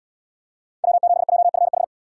Morse code, heard as a keyed tone, about 68 wpm, 700 Hz. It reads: W7JFH